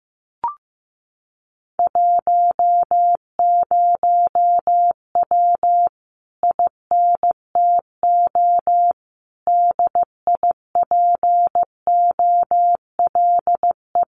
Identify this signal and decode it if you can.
Morse code, heard as a keyed tone, about 15 wpm, 700 Hz. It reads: 10W INTO DIPOLE
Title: Common QSO Phrases Encore 15wpm